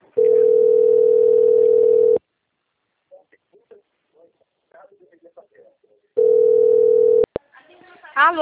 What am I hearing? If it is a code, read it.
Ringback tone (North America), 440 + 480 Hz, with speech.
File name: out- - - - - .59434